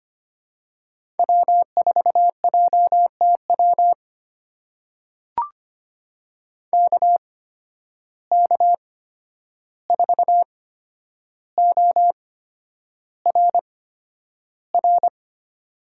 Morse code, heard as a keyed tone, about 25 wpm, spaced as for 10 wpm, 700 Hz. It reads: W4JTW KK4ORR
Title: USA Call Signs - 8x Character Spacing - 25wpm